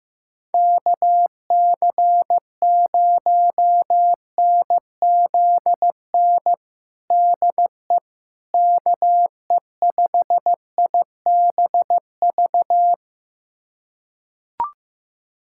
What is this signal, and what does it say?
Morse code, heard as a keyed tone, about 15 wpm, 700 Hz. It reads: KC0NZN DE KE5IBV
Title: USA Callsign DE Callsign 15wpm